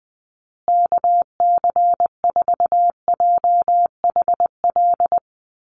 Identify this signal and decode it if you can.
Morse code, heard as a keyed tone, about 20 wpm, 700 Hz. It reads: KC4JHL
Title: USA Call Signs Repeated 2x 20wpm